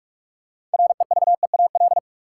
Morse code, heard as a keyed tone, about 46 wpm, 700 Hz. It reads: REVEAL
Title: Six Letter Words - 45wpm